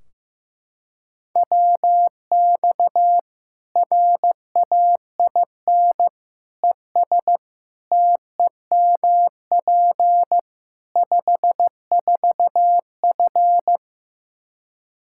Morse code, heard as a keyed tone, about 15 wpm, 700 Hz. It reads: WX RAIN ES TEMP 54F
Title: Practice QSO - Exchange 2 - Info - 15wpm